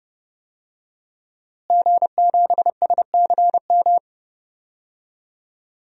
Morse code, heard as a keyed tone, about 30 wpm, 700 Hz. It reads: G7SCM
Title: DXCC-100 Call Signs - 5 Characters - Repeated 2x - 30wpm